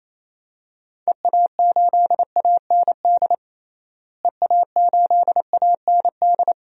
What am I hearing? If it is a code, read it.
Morse code, heard as a keyed tone, about 27 wpm, 700 Hz. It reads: EA8AND EA8AND
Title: DXCC-100 Call Signs - 6 Characters - Repeated 2x - 28wpm